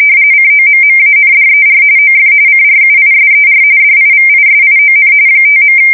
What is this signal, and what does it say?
RTTY, about 45.45 baud. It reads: COMMANDER OF CRUISER DIVISION 5.
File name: fsk170 / USS-Missouri